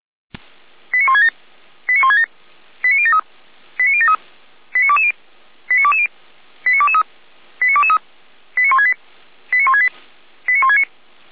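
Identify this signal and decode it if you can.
Five-tone selective call (ZVEI), heard as quick runs of five tones, tones of about 80 ms. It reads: 89167 89167 89073 89073 89220 89220 89223 89223 89168 89168 89169
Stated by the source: ZVEI five-tone selective call received on SDR, from the feuerwehr-orlamuende.org fire brigade audio server.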